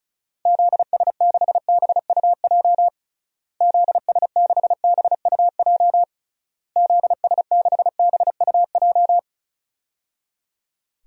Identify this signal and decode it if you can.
Morse code, heard as a keyed tone, about 36 wpm, 700 Hz. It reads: ZS6BUJ ZS6BUJ ZS6BUJ
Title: DXCC-100 Call Signs - 6 Characters - Repeated 3x - 35wpm